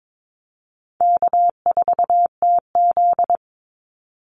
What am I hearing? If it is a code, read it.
Morse code, heard as a keyed tone, about 22 wpm, 700 Hz. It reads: K4TZ
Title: Call Signs - 22wpm